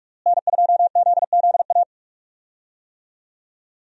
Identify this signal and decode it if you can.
Morse code, heard as a keyed tone, about 46 wpm, 700 Hz. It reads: N2ZZA
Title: USA Call Signs - 5 Characters - Repeated 2x - 45wpm